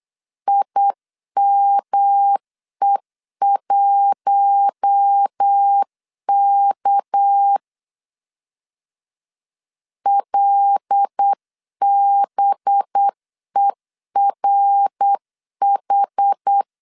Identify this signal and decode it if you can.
Morse code, heard as a keyed tone, about 8 wpm, 800 Hz. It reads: IME1K LBERH
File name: CW / DET / 1 6 1500C S C 8wpm 5219